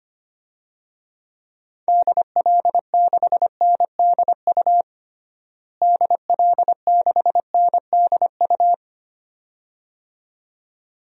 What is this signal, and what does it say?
Morse code, heard as a keyed tone, about 25 wpm, 700 Hz. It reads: DL6NDU DL6NDU
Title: DXCC-100 Call Signs - 6 Characters - Repeated 2x - 25wpm